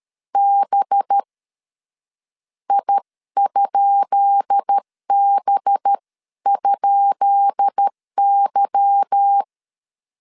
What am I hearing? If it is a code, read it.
Morse code, heard as a keyed tone, about 13 wpm, 800 Hz. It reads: B I?B?Y
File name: LEZIONI / 10 SIGNS / 10B INT 12wpm 2414